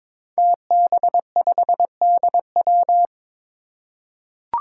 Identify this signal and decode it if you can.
Morse code, heard as a keyed tone, about 22 wpm, 700 Hz. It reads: TB5DW
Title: Five Challenging Letter-Number Combinations 22wpm